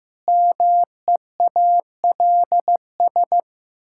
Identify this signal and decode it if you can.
Morse code, heard as a keyed tone, about 16 wpm, 700 Hz. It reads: MEALS
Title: Pangram Challenge v2 - 15wpm